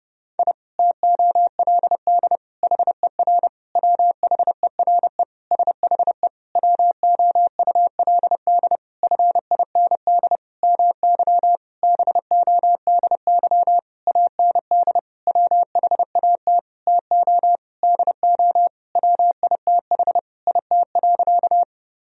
Morse code, heard as a keyed tone, about 32 wpm, 700 Hz. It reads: I TOLD HER WHERE SHE WOULD FIND MY BODY AND WHAT TO DO WITH IT.
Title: Sentences from Top 1000 Words - Review Part 2 - 30wpm